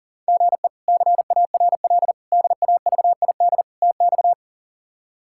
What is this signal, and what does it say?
Morse code, heard as a keyed tone, about 40 wpm, 690 Hz. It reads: GE CARL DAVID TX
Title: K1USN SST QSO Practice - 40wpm